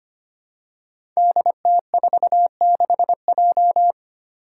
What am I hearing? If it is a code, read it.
Morse code, heard as a keyed tone, about 25 wpm, 700 Hz. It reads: DT46J
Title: Five Challenging Letter-Number Combinations 25wpm